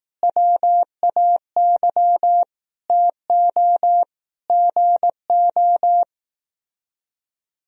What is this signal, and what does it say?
Morse code, heard as a keyed tone, about 18 wpm, 700 Hz. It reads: WAY TO GO
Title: Speed-Racing - Sets of 5 Words 15wpm